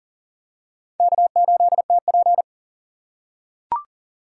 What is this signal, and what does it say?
Morse code, heard as a keyed tone, about 40 wpm, 700 Hz. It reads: K8TP
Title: USA Call Signs - 4 Characters - 6x Character Spacing - 40wpm